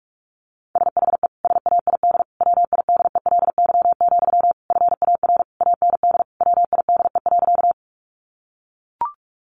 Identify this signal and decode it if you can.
Morse code, heard as a keyed tone, about 44 wpm, 700 Hz. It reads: SHE SAID WIDELY, FAR AND WIDE.
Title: Sentences from Top 1200 Words 45wpm